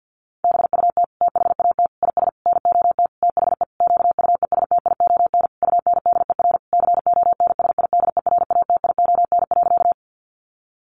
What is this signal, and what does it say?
Morse code, heard as a keyed tone, about 50 wpm, 700 Hz. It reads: BUT THAT IS NOT THE QUESTION UNDER CONSIDERATION.
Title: Sentences from Top 1500 Words 50wpm